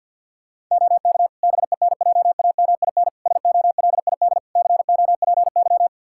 Morse code, heard as a keyed tone, about 46 wpm, 700 Hz. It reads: OK BENJAMIN SOLID COPY
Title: K1USN SST QSO Practice 50wpm